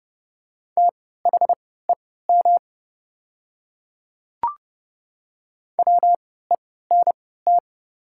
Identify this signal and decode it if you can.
Morse code, heard as a keyed tone, about 30 wpm, spaced as for 17 wpm, 700 Hz. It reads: THEM WENT
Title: 3x Character Spacing - Top 200 Words - 30wpm